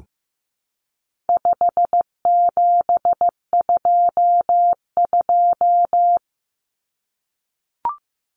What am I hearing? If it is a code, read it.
Morse code, heard as a keyed tone, about 15 wpm, 700 Hz. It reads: 5722
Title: Common Cut Numbers - 4-Digit 15wpm